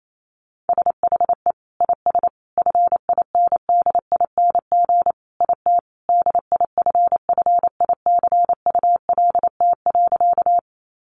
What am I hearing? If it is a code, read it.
Morse code, heard as a keyed tone, about 28 wpm, 700 Hz. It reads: SHE IS FINDING IT DIFFICULT.
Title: Sentences from Top 1100 Words 28wpm